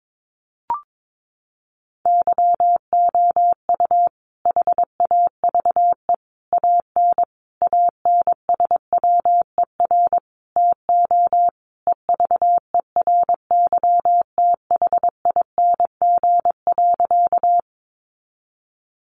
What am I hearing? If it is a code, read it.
Morse code, heard as a keyed tone, about 22 wpm, 700 Hz. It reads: YOU HAVE AN ANSWER TO EVERYTHING.
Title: Sentences from Top 700 Words 22wpm